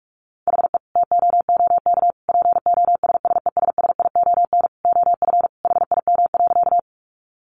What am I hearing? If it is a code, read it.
Morse code, heard as a keyed tone, about 44 wpm, 700 Hz. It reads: HE TOOK POSSESSION OF HIM.
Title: Sentences from Top 1400 Words 45wpm